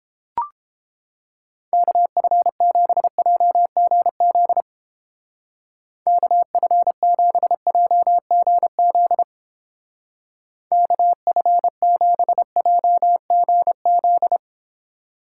Morse code, heard as a keyed tone, about 29 wpm, 700 Hz. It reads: KF7JGZ KF7JGZ KF7JGZ